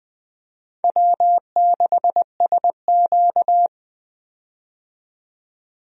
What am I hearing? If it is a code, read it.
Morse code, heard as a keyed tone, about 21 wpm, 700 Hz. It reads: W6SQ
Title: USA Call Signs - 4 Characters - 20wpm